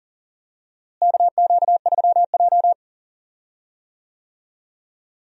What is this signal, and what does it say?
Morse code, heard as a keyed tone, about 40 wpm, 700 Hz. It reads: KQ3J
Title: USA Call Signs - 4 Characters - Repeated 3x - 40wpm